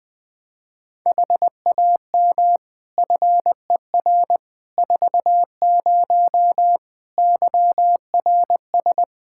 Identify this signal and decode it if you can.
Morse code, heard as a keyed tone, about 20 wpm, 700 Hz. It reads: HAM FER 40 YRS
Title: Practice QSO - Exchange 3 - More Info - 20wpm